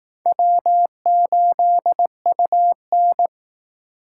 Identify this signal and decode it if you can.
Morse code, heard as a keyed tone, about 18 wpm, 700 Hz. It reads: W8UN
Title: Speed Racing - USA Call Signs 15wpm